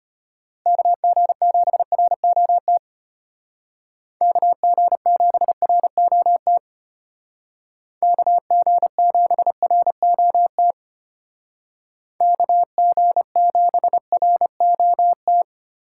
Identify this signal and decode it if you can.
Morse code, heard as a keyed tone, about 31 wpm, 700 Hz. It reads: KG7ROT KG7ROT KG7ROT KG7ROT